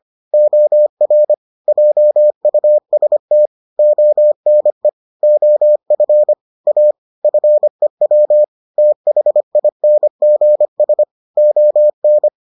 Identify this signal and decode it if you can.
Morse code, heard as a keyed tone, about 25 wpm, 600 Hz. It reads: OR JUST ONE OF A FEW THINGS ON